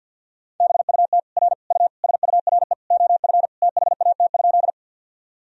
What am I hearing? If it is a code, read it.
Morse code, heard as a keyed tone, about 51 wpm, 700 Hz. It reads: BUT R U SURE OF THAT?
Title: Sentences from Top 300 Words and Abbreviations 50wpm